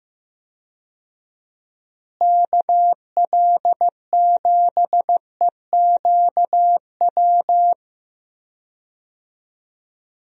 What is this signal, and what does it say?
Morse code, heard as a keyed tone, about 15 wpm, 700 Hz. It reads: KL7EQW
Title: DXCC-100 Call Signs - 6 Characters - Repeated 3x - 15wpm